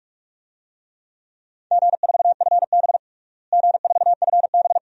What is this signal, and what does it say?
Morse code, heard as a keyed tone, about 46 wpm, 700 Hz. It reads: G4FB G4FB
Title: DXCC-100 Call Signs - 4 Characters - Repeated 2x - 45wpm